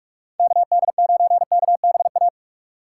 Morse code, heard as a keyed tone, about 46 wpm, 700 Hz. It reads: KD9KBA
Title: USA Call Signs - 6 Characters - 4x Character Spacing - 45wpm